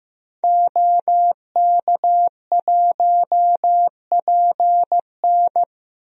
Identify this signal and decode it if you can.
Morse code, heard as a keyed tone, about 15 wpm, 700 Hz. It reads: OK1PN
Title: DXCC-100 Call Signs - 5 Characters - 15wpm